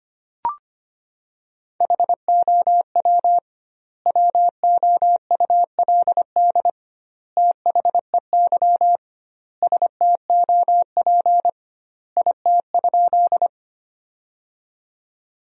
Morse code, heard as a keyed tone, about 25 wpm, 700 Hz. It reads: HOW WOULD THEY STOP IT?
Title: 2x Word-Spacing - Sentences from Top 300 Words - 25wpm